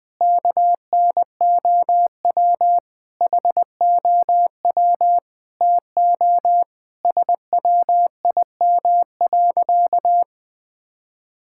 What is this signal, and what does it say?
Morse code, heard as a keyed tone, about 20 wpm, 700 Hz. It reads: KNOW HOW TO SWIM.
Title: Sentences with 5-Word Phrases - Spoken Beginning - 20wpm